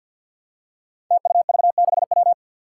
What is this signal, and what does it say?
Morse code, heard as a keyed tone, about 47 wpm, 700 Hz. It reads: TUV6W
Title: Five Challenging Letter-Number Combinations 50wpm